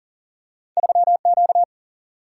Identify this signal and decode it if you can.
Morse code, heard as a keyed tone, about 38 wpm, 700 Hz. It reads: 3Q
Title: Single-Letter Single-Number 40wpm